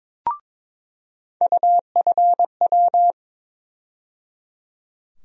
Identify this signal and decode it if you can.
Morse code, heard as a keyed tone, about 23 wpm, 700 Hz. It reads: UFW
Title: USA Call Sign Postfixes - 3 Letters 22wpm